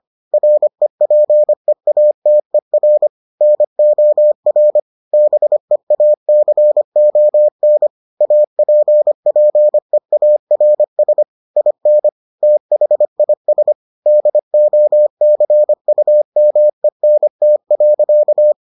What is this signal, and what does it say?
Morse code, heard as a keyed tone, about 25 wpm, 600 Hz. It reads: REPEATER NOR BEACON APPEARS IN THIS DOCUMENT.